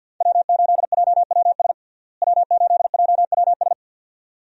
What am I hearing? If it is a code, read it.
Morse code, heard as a keyed tone, about 52 wpm, 700 Hz. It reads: W8JWS W8JWS
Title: USA Call Signs - 5 Characters - Repeated 2x - 50wpm